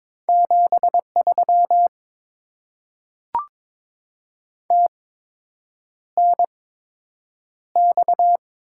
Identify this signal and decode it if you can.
Morse code, heard as a keyed tone, about 22 wpm, spaced as for 8 wpm, 700 Hz. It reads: 73 TNX
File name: Top 20 QSO Elements - 8x Character Spacing - 22wpm